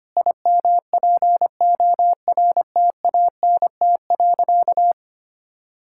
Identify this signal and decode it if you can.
Morse code, heard as a keyed tone, about 25 wpm, 700 Hz. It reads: IMPORTANT.
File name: practice / 25wpm / Sentences with 4-Word Phrases - 25wpm